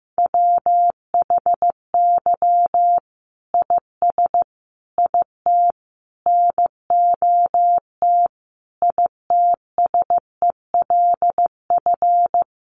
Morse code, heard as a keyed tone, about 15 wpm, 700 Hz. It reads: WHY IS IT NOT ITSELF
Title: Sentences from Top 1400 Words 15wpm